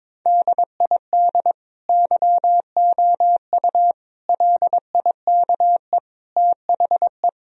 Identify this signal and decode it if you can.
Morse code, heard as a keyed tone, about 22 wpm, 700 Hz. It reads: DID YOU LIKE THE
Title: Sentences from Top 1000 Words - Review Part 4 - 22wpm